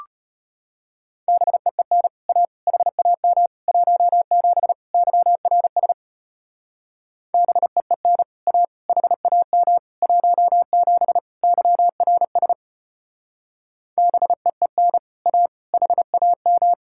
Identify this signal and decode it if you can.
Morse code, heard as a keyed tone, about 31 wpm, 700 Hz. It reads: BEEN A HAM 17 YRS BEEN A HAM 17 YRS BEEN A HAM
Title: Speed-Racing Common QSO Phrases - 25wpm